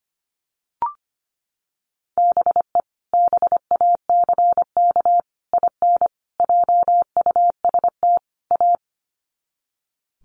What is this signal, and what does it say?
Morse code, heard as a keyed tone, about 25 wpm, 700 Hz. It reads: BE BACK IN JUST A
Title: Sets of 5 Words Encore 25wpm